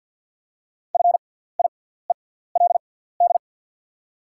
Morse code, heard as a keyed tone, about 48 wpm, spaced as for 17 wpm, 690 Hz. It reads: FIELD
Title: 6x Character Spacing - Top 500 Words - 50wpm